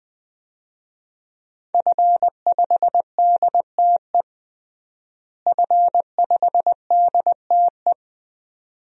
Morse code, heard as a keyed tone, about 21 wpm, 700 Hz. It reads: F5DTE F5DTE
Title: DXCC-100 Call Signs - 5 Characters - Repeated 2x - 20wpm